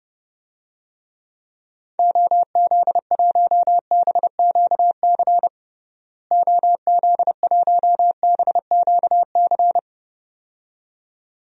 Morse code, heard as a keyed tone, about 30 wpm, 700 Hz. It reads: OZ1BQC OZ1BQC
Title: DXCC-100 Call Signs - 6 Characters - Repeated 2x - 30wpm